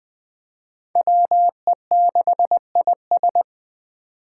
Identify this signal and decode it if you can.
Morse code, heard as a keyed tone, about 20 wpm, 700 Hz. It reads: WE6IS
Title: Five Challenging Letter-Number Combinations 20wpm